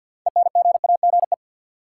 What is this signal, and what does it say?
Morse code, heard as a keyed tone, about 53 wpm, 700 Hz. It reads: ENGAGE
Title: Sentences from Top 1500 Words - New Word Drill - 50wpm